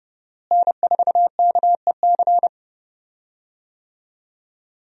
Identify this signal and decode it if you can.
Morse code, heard as a keyed tone, about 30 wpm, 700 Hz. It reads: N4KEC
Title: USA Call Signs - 5 Characters - Repeated 3x - 30wpm